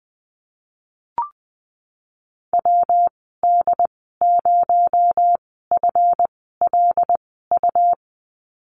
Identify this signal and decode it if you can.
Morse code, heard as a keyed tone, about 19 wpm, spaced as for 14 wpm, 700 Hz. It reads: WD0FLU
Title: 238 Lesson 38g - Letter q - Call Signs 20wpm